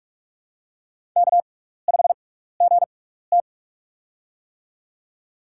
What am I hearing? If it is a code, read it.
Morse code, heard as a keyed tone, about 44 wpm, spaced as for 15 wpm, 700 Hz. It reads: K5GT
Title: USA Call Signs - 6x Character Spacing - 45wpm